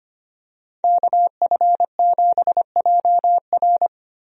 Morse code, heard as a keyed tone, about 25 wpm, 700 Hz. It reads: KF7JR
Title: USA Call Signs - 5 Characters - 25wpm